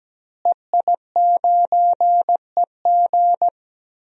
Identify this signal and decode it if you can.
Morse code, heard as a keyed tone, about 17 wpm, 700 Hz. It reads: EI9EG
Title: DXCC-100 Call Signs - 5 Characters - Repeated 3x - 17wpm